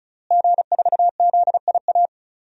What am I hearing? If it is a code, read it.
Morse code, heard as a keyed tone, about 36 wpm, 700 Hz. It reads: G4ZIA
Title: DXCC-100 Call Signs - 5 Characters - Repeated 3x - 35wpm